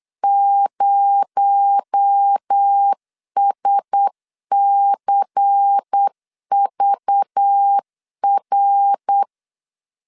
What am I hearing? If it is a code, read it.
Morse code, heard as a keyed tone, about 8 wpm, 800 Hz. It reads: 0SCVR
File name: CW / DET / 1 6 1500C S B 8wpm 5142